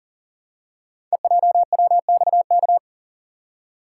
Morse code, heard as a keyed tone, about 38 wpm, 700 Hz. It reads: EJWXK
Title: Code Groups - 5 Letters - 40wpm